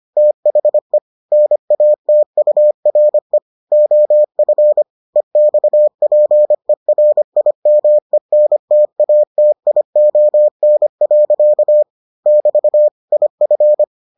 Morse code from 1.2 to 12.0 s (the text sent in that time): NATURE OF EXPERIMENTATION.